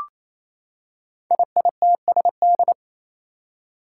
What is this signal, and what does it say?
Morse code, heard as a keyed tone, about 27 wpm, 700 Hz. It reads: IITSD
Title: Code Groups - 5 Letters - 28wpm